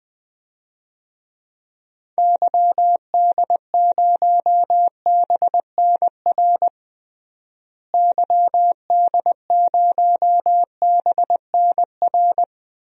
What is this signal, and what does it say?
Morse code, heard as a keyed tone, about 20 wpm, 700 Hz. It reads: YD0BNR YD0BNR
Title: DXCC-100 Call Signs - 6 Characters - Repeated 3x - 20wpm